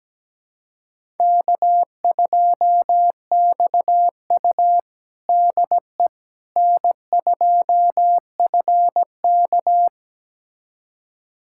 Morse code, heard as a keyed tone, about 17 wpm, 700 Hz. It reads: K2XU DE N2FK
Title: USA Callsign DE Callsign - 4 Characters - 17wpm